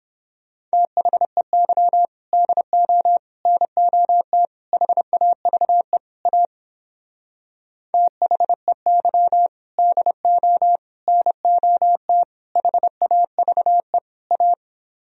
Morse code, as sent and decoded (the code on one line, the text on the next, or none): - .... -.-- / -.. --- / -. --- - / .... .- ...- . / .- / - .... -.-- / -.. --- / -. --- - / .... .- ...- . / .-
THEY DO NOT HAVE A THEY DO NOT HAVE A